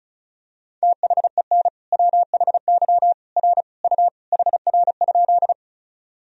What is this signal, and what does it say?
Morse code, heard as a keyed tone, about 36 wpm, 700 Hz. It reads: THEN WHY R U HR?
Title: Sentences from Top 200 Words and Abbreviations 35wpm